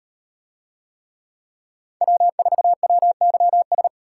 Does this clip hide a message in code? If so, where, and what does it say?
Yes: Morse code from 2.0 to 3.9 s, heard as a keyed tone, about 38 wpm, 700 Hz. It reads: W4WYS